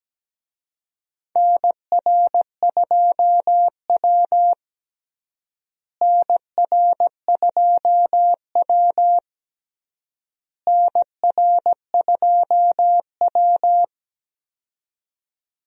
Morse code, heard as a keyed tone, about 17 wpm, 700 Hz. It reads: NR2W NR2W NR2W
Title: USA Call Signs - 4 Characters - Repeated 3x - 17wpm